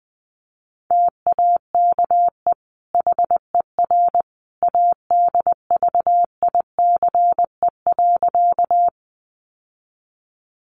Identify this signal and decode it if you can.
Morse code, heard as a keyed tone, about 20 wpm, 700 Hz. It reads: TAKE HER ADVICE.
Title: Sentences from Top 1100 Words 20wpm